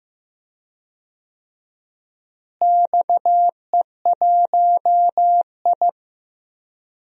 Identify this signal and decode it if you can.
Morse code, heard as a keyed tone, about 15 wpm, 700 Hz. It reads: XE1I
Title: DXCC-100 Call Signs - 4 Characters - Repeated 3x - 15wpm